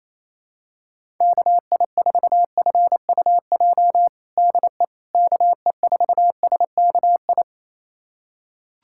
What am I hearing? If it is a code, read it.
Morse code, heard as a keyed tone, about 27 wpm, 700 Hz. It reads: KI4FUJ DE KE4SKI
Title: USA Callsign DE Callsign - 6 Characters - 28wpm